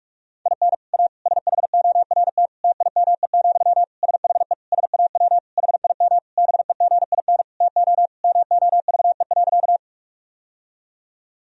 Morse code, heard as a keyed tone, about 41 wpm, 700 Hz. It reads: IN A SHORT TIME, SHE SAW HIM BEGIN TO MOVE.
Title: Sentences from Top 1000 Words - Review Part 2 - 45wpm